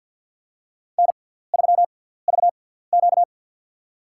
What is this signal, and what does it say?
Morse code, heard as a keyed tone, about 54 wpm, spaced as for 17 wpm, 700 Hz. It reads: N3VQ
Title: USA Call Signs - 4 Characters - 6x Character Spacing - 50wpm